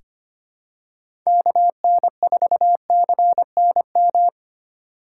Morse code, heard as a keyed tone, about 25 wpm, 700 Hz. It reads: KN4CNM
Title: USA Call Signs - 4x Character Spacing - 25wpm